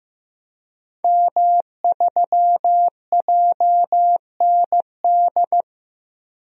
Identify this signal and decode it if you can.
Morse code, heard as a keyed tone, about 15 wpm, 700 Hz. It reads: M3JND
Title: DXCC-100 Call Signs - 5 Characters - Repeated 3x - 15wpm